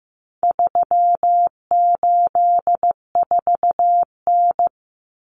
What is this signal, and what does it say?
Morse code, heard as a keyed tone, about 15 wpm, 700 Hz. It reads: 384N
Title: Common Cut Numbers - 4-Digit 15wpm